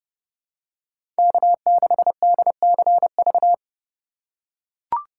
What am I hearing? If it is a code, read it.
Morse code, heard as a keyed tone, about 30 wpm, 700 Hz. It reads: K6DCV
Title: USA Call Signs - 5 Characters - 2x Character Spacing - 30wpm